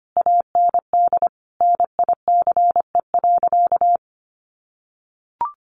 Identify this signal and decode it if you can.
Morse code, heard as a keyed tone, about 25 wpm, 700 Hz. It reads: AND NICE.
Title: Sentences from Top 1200 Words 25wpm